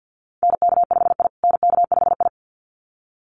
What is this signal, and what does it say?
Morse code, heard as a keyed tone, about 50 wpm, 700 Hz. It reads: NK5I NK5I
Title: USA Call Signs Repeated 2x 50wpm